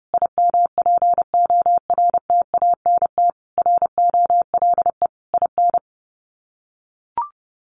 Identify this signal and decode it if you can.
Morse code, heard as a keyed tone, about 30 wpm, 700 Hz. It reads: IMPORTANT ROLE IN
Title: Sets of 5 Words Encore2 30wpm